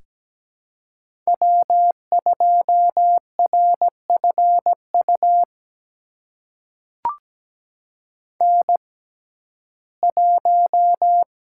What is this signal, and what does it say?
Morse code, heard as a keyed tone, about 17 wpm, spaced as for 12 wpm, 700 Hz. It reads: W2RFU N1
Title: USA Call Signs - 5 Characters - 6x Character Spacing - 17wpm